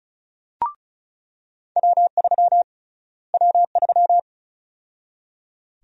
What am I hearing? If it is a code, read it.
Morse code, heard as a keyed tone, about 36 wpm, 700 Hz. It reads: W3 W3
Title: USA Call Sign Prefixes - Repeated 2x - 35wpm